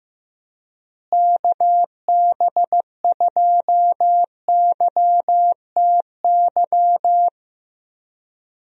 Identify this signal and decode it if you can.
Morse code, heard as a keyed tone, about 15 wpm, 700 Hz. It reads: KB2YTY
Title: USA Call Signs - 6 Characters - Repeated 3x - 15wpm